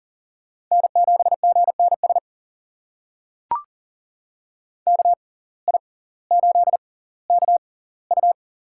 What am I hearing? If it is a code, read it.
Morse code, heard as a keyed tone, about 40 wpm, spaced as for 20 wpm, 700 Hz. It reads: N7GNS KI8KU
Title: USA Call Signs - 5 Characters - 6x Character Spacing - 40wpm